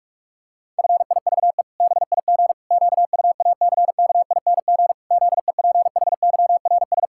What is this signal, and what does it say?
Morse code, heard as a keyed tone, about 46 wpm, 700 Hz. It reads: FIVE BIG QUACKING ZEPHYRS
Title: Pangram Challenge v2 - 45wpm